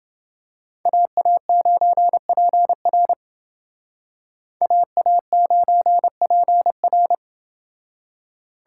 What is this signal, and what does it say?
Morse code, heard as a keyed tone, about 28 wpm, 700 Hz. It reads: AA9PR AA9PR